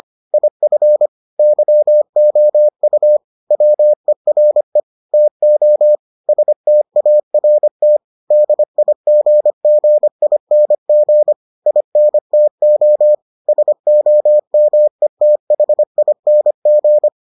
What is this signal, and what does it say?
Morse code, heard as a keyed tone, about 25 wpm, 600 Hz. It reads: IF YOU WERE TO START DIGGING INTO SOMETHING